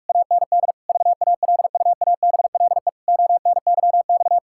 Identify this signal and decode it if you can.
Morse code, heard as a keyed tone, about 41 wpm, 700 Hz. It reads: AND VALUABLE ONYX
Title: Pangram Challenge v2 - 45wpm